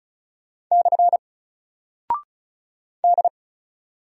Morse code, heard as a keyed tone, about 36 wpm, 700 Hz. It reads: / D
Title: Single Character (Letter:Number:Punctuation:Prosign) 35wpm